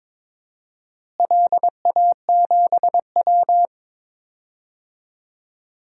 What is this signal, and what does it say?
Morse code, heard as a keyed tone, about 23 wpm, 700 Hz. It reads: LA7W
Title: DXCC-100 Call Signs - 4 Characters - Repeated 3x - 22wpm